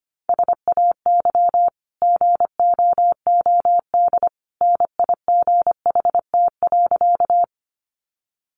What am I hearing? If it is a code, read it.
Morse code, heard as a keyed tone, about 25 wpm, 700 Hz. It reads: SAY GOOD NIGHT.
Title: Sentences from Top 300 Words 25wpm